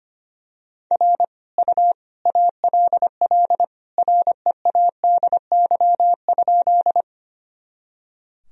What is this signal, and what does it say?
Morse code, heard as a keyed tone, about 25 wpm, 700 Hz. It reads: R U ALL READY?
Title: Sentences from Top 400 Words and Abbreviations 25wpm